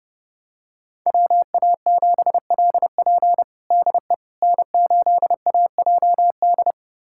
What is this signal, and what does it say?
Morse code, heard as a keyed tone, about 30 wpm, 700 Hz. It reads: WA7LP DE N8AJD
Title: USA Callsign DE Callsign - 5 Characters - 30wpm